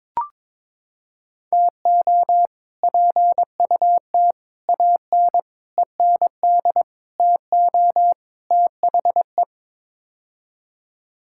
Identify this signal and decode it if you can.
Morse code, heard as a keyed tone, about 22 wpm, 700 Hz. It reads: TO PUT AN END TO THE
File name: practice / 22wpm / Sets of 6 Words - 22wpm